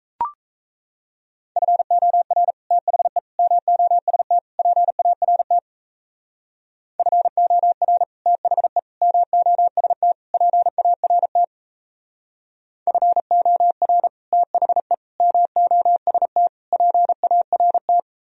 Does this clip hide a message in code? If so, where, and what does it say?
Yes: Morse code from 1.6 to 18.0 s, heard as a keyed tone, about 38 wpm, 700 Hz. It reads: FOR THE MOST PART FOR THE MOST PART FOR THE MOST PART